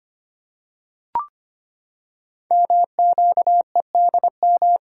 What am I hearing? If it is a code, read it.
Morse code, heard as a keyed tone, about 25 wpm, 700 Hz. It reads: MQEDM
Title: Code Groups - 5 Letter:Numbers - 25wpm